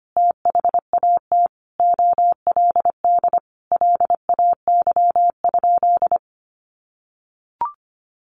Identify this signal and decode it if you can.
Morse code, heard as a keyed tone, about 25 wpm, 700 Hz. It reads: THAT OLD LAY?
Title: Sentences from Top 400 Words 25wpm